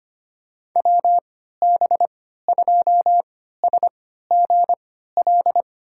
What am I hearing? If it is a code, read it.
Morse code, heard as a keyed tone, about 25 wpm, spaced as for 14 wpm, 700 Hz. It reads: WB2SGL
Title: USA Call Signs - 3x Character Spacing - 25wpm